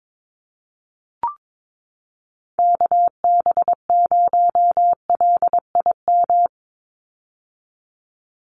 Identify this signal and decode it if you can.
Morse code, heard as a keyed tone, about 22 wpm, 700 Hz. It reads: KB0LIM
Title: USA Call Signs 22wpm